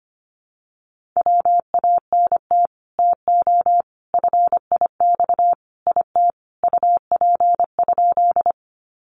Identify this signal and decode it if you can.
Morse code, heard as a keyed tone, about 25 wpm, 700 Hz. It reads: WANT TO FIX IT UP?
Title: Sentences from Top 1400 Words 25wpm